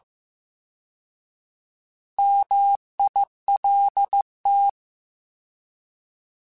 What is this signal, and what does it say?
Morse code, heard as a keyed tone, about 15 wpm, 800 Hz. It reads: MILT
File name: Alphabetical / Alphabetical mp3s / Ms @13